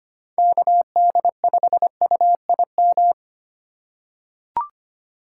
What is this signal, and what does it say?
Morse code, heard as a keyed tone, about 25 wpm, 700 Hz. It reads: KD5UIM
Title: USA Call Signs - 6 Characters - 4x Character Spacing - 25wpm